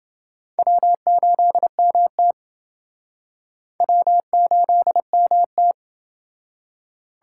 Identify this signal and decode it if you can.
Morse code, heard as a keyed tone, about 28 wpm, 700 Hz. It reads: W8MT W8MT